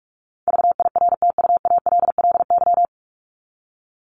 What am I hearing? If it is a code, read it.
Morse code, heard as a keyed tone, about 44 wpm, 700 Hz. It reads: VIRTUALLY
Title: LY Suffix - Top 100 Words with Base Words 45wpm